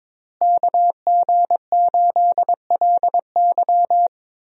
Morse code, heard as a keyed tone, about 22 wpm, 700 Hz. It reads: KG8LY
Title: USA Call Signs - 5 Characters - Repeated 3x - 22wpm